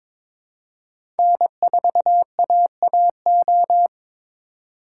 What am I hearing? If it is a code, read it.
Morse code, heard as a keyed tone, about 22 wpm, 700 Hz. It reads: N4AAO
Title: USA Call Signs - 5 Characters - 3x Character Spacing - 22wpm